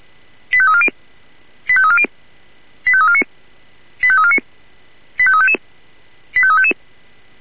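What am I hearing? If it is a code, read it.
Five-tone selective call (ZVEI), heard as quick runs of five tones, tones of about 70 ms. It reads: 85340 85340 85349 85349 85355 85355
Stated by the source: ZVEI five-tone call 85340 received on SDR.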